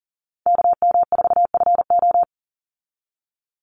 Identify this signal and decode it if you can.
Morse code, heard as a keyed tone, about 40 wpm, 700 Hz. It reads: KM4FO
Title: Call Signs Repeated 3x - 40wpm